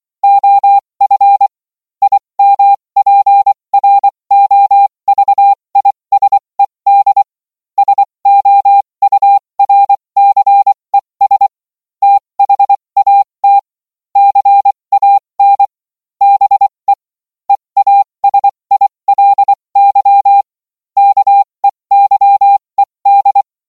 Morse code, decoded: OF IMPROVISED SOURCES THAT CAN BE EASILY KEYED